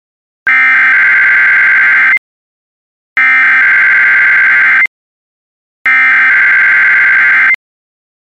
Emergency Alert System (SAME) tones (header bursts), heard as harsh buzzing bursts. It is heard from 0.5 to 7.5 s.
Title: Relaying Required Weekly Test from WJON/TV.